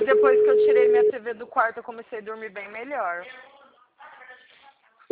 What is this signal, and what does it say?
Ringback tone (Europe, 425 Hz), with speech.